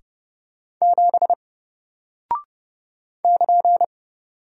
Morse code, heard as a keyed tone, about 31 wpm, 700 Hz. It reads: 7 <KN>
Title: Single Character (Letter:Number:Punctuation:Prosign) Alt 30wpm